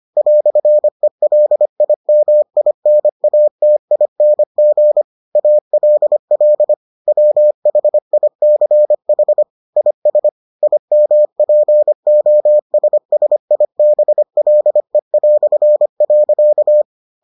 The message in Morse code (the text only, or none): "ELIMINATING ALL WHICH IS IMPOSSIBLE".